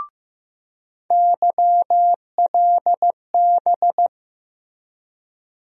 Morse code, heard as a keyed tone, about 15 wpm, 700 Hz. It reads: YLB
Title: USA Call Sign Postfixes - 3 Letters 15wpm